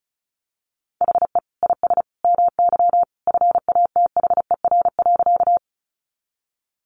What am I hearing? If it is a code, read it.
Morse code, heard as a keyed tone, about 34 wpm, 700 Hz. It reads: HE IS MY FATHER.
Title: Sentences from Top 300 Words 35wpm